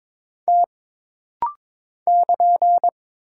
Morse code, heard as a keyed tone, about 22 wpm, 700 Hz. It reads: T <KN>
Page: This rapid-fire timing is geared to push your ICR (Instant Character Recognition) skills.